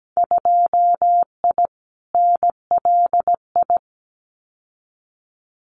Morse code, heard as a keyed tone, about 17 wpm, 700 Hz. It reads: 2I NLI
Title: Winter Field Day Exchanges 17wpm